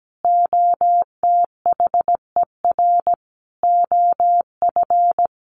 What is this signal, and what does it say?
Morse code, heard as a keyed tone, about 17 wpm, 700 Hz. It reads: OTHER OF
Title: Sentences from Top 400 Words 17wpm